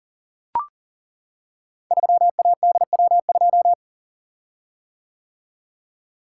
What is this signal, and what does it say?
Morse code, heard as a keyed tone, about 41 wpm, 700 Hz. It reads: 3ADW2
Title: Code Groups - 5 Letter:Numbers - 40wpm